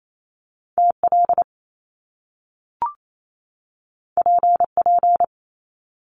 Morse code, heard as a keyed tone, about 28 wpm, 700 Hz. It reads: TL PP
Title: Top Two Letter Combinations in Words 28wpm